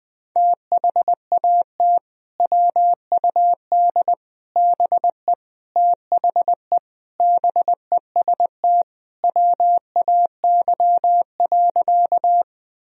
Morse code, heard as a keyed tone, about 20 wpm, 700 Hz. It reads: THAT WUD BE THE BEST WAY.
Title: Sentences from Top 400 Words and Abbreviations 20wpm